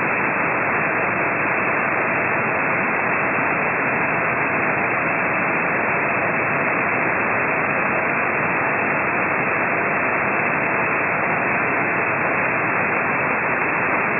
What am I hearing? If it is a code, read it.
FT8.
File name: ft8-vs-cw / 25-to-150 FT8 -17